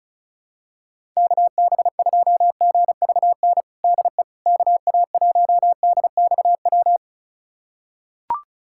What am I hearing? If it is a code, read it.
Morse code, heard as a keyed tone, about 34 wpm, 700 Hz. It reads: KB2GVN DE KA1DXW
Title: USA Callsign DE Callsign - 6 Characters - 35wpm